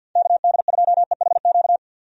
Morse code, heard as a keyed tone, about 51 wpm, 700 Hz. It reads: KD2EHQ
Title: USA Call Signs - 6 Characters - 50wpm